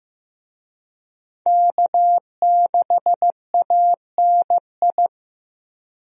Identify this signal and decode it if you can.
Morse code, heard as a keyed tone, about 15 wpm, 700 Hz. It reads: K6ANI
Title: USA Call Signs - 5 Characters - Repeated 3x - 15wpm